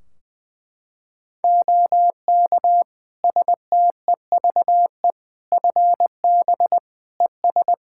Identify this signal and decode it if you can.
Morse code, heard as a keyed tone, about 20 wpm, 700 Hz. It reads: OK STEVE FB ES
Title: Practice QSO - Exchange 2 - Info - 20wpm